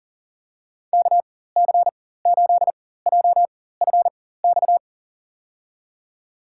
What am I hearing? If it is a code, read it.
Morse code, heard as a keyed tone, about 40 wpm, spaced as for 19 wpm, 700 Hz. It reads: KC8JFX
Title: USA Call Signs - 6 Characters - 4x Character Spacing - 40wpm